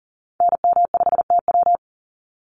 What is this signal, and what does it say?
Morse code, heard as a keyed tone, about 40 wpm, 700 Hz. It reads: NM5TW
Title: NA SOTA Activator Callsigns 40wpm